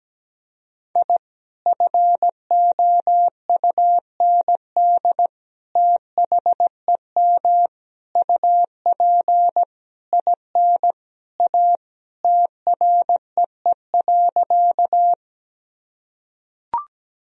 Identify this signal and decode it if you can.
Morse code, heard as a keyed tone, about 17 wpm, 700 Hz. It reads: I FOUND THEM UP IN A TREE.
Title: Sentences from Top 1000 Words - Review Part 1 - 17wpm